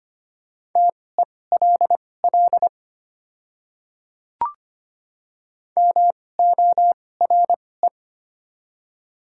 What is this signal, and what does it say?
Morse code, heard as a keyed tone, about 25 wpm, spaced as for 18 wpm, 690 Hz. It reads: TELL MORE
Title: 2x Character Spacing - Top 200 Words - 25wpm